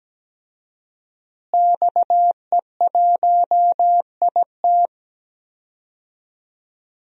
Morse code, heard as a keyed tone, about 17 wpm, 700 Hz. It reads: XE1IT
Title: DXCC-100 Call Signs - 5 Characters - Repeated 2x - 17wpm